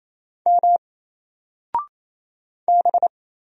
Morse code, heard as a keyed tone, about 28 wpm, 700 Hz. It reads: M B
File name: Single-Letter-Number-28wpm